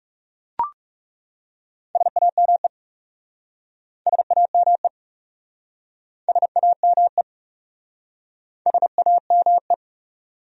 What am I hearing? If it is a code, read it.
Morse code, heard as a keyed tone, about 37 wpm, 700 Hz. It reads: SAME SAME SAME SAME
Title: Speed-Racing Top 300 Words 30wpm